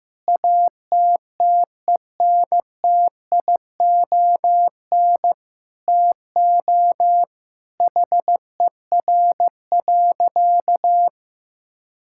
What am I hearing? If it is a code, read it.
Morse code, heard as a keyed tone, about 15 wpm, 700 Hz. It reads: ATTENTION TO HER.
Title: Sentences from Top 1000 Words - Review Part 6 - 15wpm